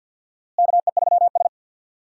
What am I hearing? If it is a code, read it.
Morse code, heard as a keyed tone, about 48 wpm, 700 Hz. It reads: KE3S
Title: USA Call Signs - 4 Characters - Repeated 2x - 50wpm